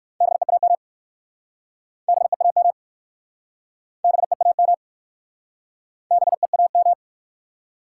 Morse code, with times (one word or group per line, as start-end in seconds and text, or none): 0.2-0.7 BEAM
2.1-2.7 BEAM
4.0-4.7 BEAM
6.1-6.9 BEAM